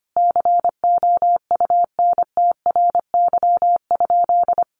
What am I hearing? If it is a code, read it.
Morse code, heard as a keyed tone, about 25 wpm, 700 Hz. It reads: COUNTRY?
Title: Sentences from Top 1400 Words 25wpm